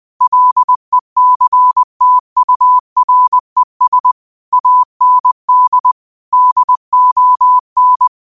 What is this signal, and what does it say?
Morse code, heard as a keyed tone, about 20 wpm, 1000 Hz. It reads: LECTURES AND DON
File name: qotdfiles / 20wpm